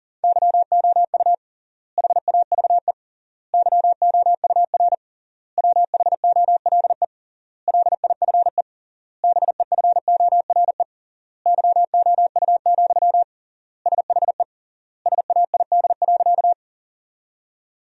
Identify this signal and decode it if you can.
Morse code, heard as a keyed tone, about 37 wpm, 700 Hz. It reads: YOU HAVE YOUR WHOLE LIFE BEFORE YOU, SHE SAID.
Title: 3x Word-Spacing - Sentences from Top 400 Words - 40wpm